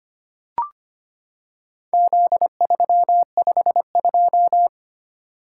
Morse code, heard as a keyed tone, about 25 wpm, 700 Hz. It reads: Z352
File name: practice / 25wpm / Four Letter-Number Combinations - 25wpm